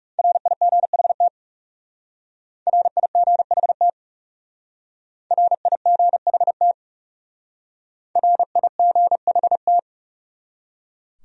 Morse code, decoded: RIGHT RIGHT RIGHT RIGHT

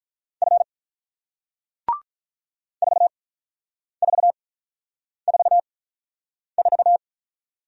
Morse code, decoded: F 4444

